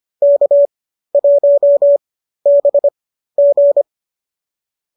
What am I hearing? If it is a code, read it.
Morse code, heard as a keyed tone, about 25 wpm, spaced as for 13 wpm, 570 Hz. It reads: K1BG